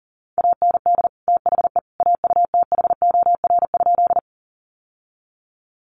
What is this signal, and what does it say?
Morse code, heard as a keyed tone, about 40 wpm, 700 Hz. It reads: AND THE AUTHOR?
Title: Sentences from Top 1200 Words 40wpm